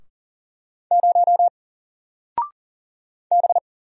Morse code, heard as a keyed tone, about 40 wpm, 700 Hz. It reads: 0 B
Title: Single Letter-Number Alt 40wpm